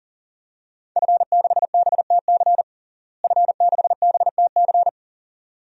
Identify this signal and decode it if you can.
Morse code, heard as a keyed tone, about 40 wpm, 700 Hz. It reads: F6BTC F6BTC
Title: DXCC-100 Call Signs - 5 Characters - Repeated 2x - 40wpm